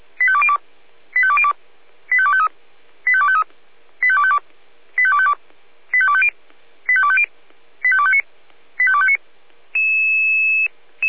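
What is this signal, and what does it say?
Five-tone selective call (ZVEI), heard as quick runs of five tones, tones of about 70 ms. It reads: 85202 85202 85253 85253 85252 85252 85250 85250 85259 85259
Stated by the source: ZVEI five-tone selective call 85102 received on an SDR.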